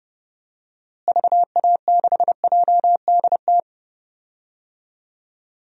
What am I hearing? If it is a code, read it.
Morse code, heard as a keyed tone, about 31 wpm, 700 Hz. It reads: VA6JDT
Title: DXCC-100 Call Signs - 6 Characters - Repeated 3x - 30wpm